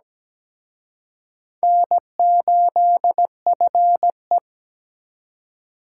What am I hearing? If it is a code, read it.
Morse code, heard as a keyed tone, about 20 wpm, 700 Hz. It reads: N8FE
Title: Speed Racing - USA 1x2 Call Signs 17wpm